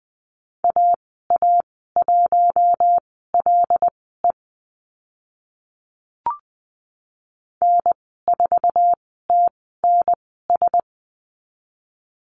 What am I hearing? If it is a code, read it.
Morse code, heard as a keyed tone, about 20 wpm, spaced as for 14 wpm, 700 Hz. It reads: AA1LE N4TNS
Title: 076 Lesson 13g - Letter L - Call Signs 20wpm